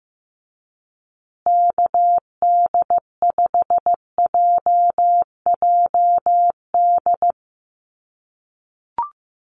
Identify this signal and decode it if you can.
Morse code, heard as a keyed tone, about 15 wpm, 700 Hz. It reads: KD5JJD
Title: USA Call Signs Repeated 3x 15wpm